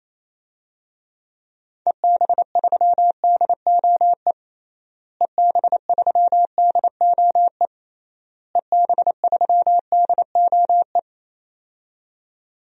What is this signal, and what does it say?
Morse code, heard as a keyed tone, about 28 wpm, 700 Hz. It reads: EB3DOE EB3DOE EB3DOE
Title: DXCC-100 Call Signs - 6 Characters - Repeated 3x - 28wpm